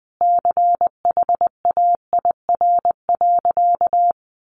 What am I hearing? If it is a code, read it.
Morse code, heard as a keyed tone, about 20 wpm, 700 Hz. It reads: CHAIR.
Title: A Sound Of Thunder 20wpm